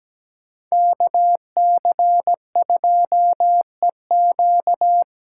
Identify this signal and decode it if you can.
Morse code, heard as a keyed tone, about 17 wpm, 700 Hz. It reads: KC2EQ
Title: USA Callsign DE Callsign - 5 Characters - 17wpm